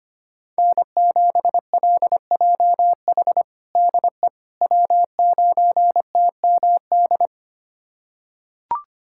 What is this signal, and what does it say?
Morse code, heard as a keyed tone, about 25 wpm, 700 Hz. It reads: N7LJH DE W9TMD
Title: USA Callsign DE Callsign - 5 Characters - 25wpm